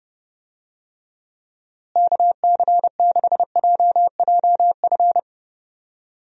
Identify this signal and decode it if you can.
Morse code, heard as a keyed tone, about 30 wpm, 700 Hz. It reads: KC6JJF
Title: USA Call Signs - 6 Characters - Repeated 2x - 30wpm